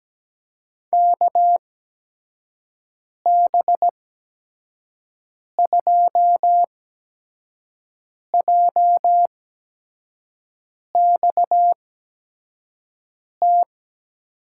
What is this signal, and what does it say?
Morse code, heard as a keyed tone, about 17 wpm, spaced as for 5 wpm, 700 Hz. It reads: KB2JXT
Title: USA Call Signs - 8x Character Spacing - 17wpm